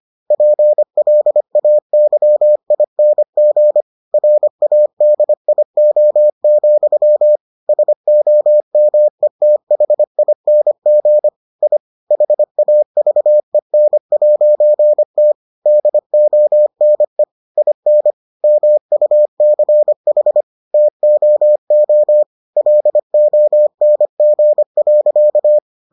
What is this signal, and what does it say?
Morse code, heard as a keyed tone, about 25 wpm, 600 Hz. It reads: PLAYING RADIO, SOMETHING I HAVEN'T DONE IN MUCH TOO LONG.